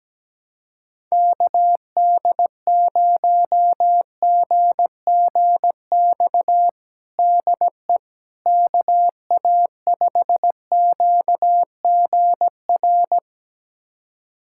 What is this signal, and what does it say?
Morse code, heard as a keyed tone, about 17 wpm, 700 Hz. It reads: KD0GGX DE KA5QGR
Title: USA Callsign DE Callsign 17wpm